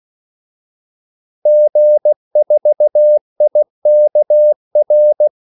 Morse code, heard as a keyed tone, about 16 wpm, 600 Hz. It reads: G4IKR
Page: Kaldesignaler 16 wpm